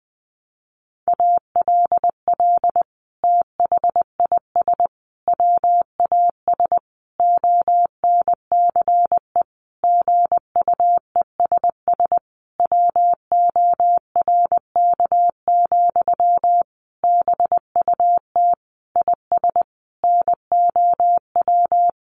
Morse code, heard as a keyed tone, about 20 wpm, 700 Hz. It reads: ALL THIS WAS ONCE GUESS WORK, BUT IS NOW
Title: Sentences from Top 1500 Words 20wpm